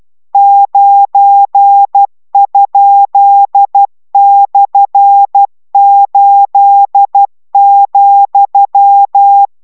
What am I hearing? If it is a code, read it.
Morse code, heard as a keyed tone, about 12 wpm, 800 Hz. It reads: 9?/8,